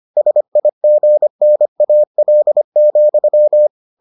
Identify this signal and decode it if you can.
Morse code, heard as a keyed tone, about 25 wpm, 600 Hz. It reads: SIGNAL,